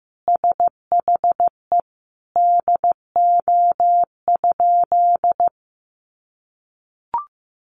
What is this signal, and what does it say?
Morse code, heard as a keyed tone, about 15 wpm, 700 Hz. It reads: SHE DO?
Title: Sentences from Top 800 Words 15wpm